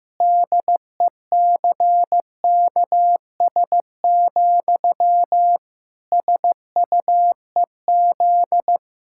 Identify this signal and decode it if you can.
Morse code, heard as a keyed tone, about 15 wpm, 700 Hz. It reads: DECKS, SUEZ
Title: Pangram Challenge v2 - 15wpm